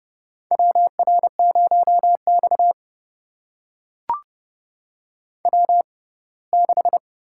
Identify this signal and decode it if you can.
Morse code, heard as a keyed tone, about 30 wpm, spaced as for 20 wpm, 700 Hz. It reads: WR0X W6
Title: USA Call Signs - 6x Character Spacing - 30wpm